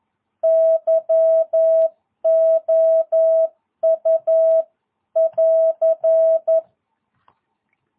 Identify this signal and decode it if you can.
Morse code, heard as a keyed tone, about 11 wpm, 660 Hz. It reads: YOU <AR>
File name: jh8cbh / 2022cw